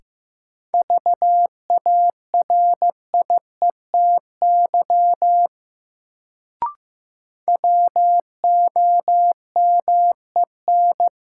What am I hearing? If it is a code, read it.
Morse code, heard as a keyed tone, about 15 wpm, 700 Hz. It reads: VARIETY WOMEN
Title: Sets of 3 Words - New Word Drill - 15wpm